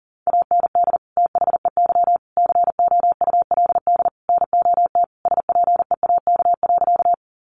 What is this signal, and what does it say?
Morse code, heard as a keyed tone, about 40 wpm, 700 Hz. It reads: AND THEY COULD NOT SPEAK.
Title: Sentences from Top 1100 Words 40wpm